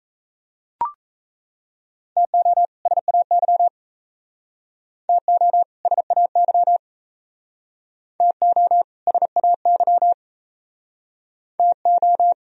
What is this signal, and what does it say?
Morse code, heard as a keyed tone, about 34 wpm, 700 Hz. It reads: TO SAY TO SAY TO SAY TO
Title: Speed-Racing - Sets of 2 Words 28wpm